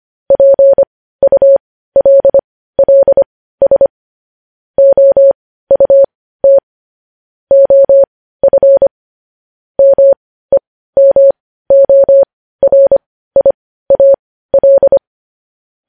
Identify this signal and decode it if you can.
Morse code, heard as a keyed tone, about 25 wpm, spaced as for 15 wpm, 550 Hz. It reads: PULLS OUT OF MEMORIAL
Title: News Headlines at 15